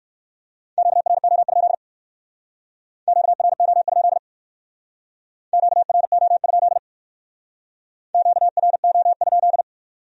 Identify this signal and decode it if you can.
Morse code, heard as a keyed tone, about 65 wpm, 700 Hz. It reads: QRO? QRO? QRO? QRO?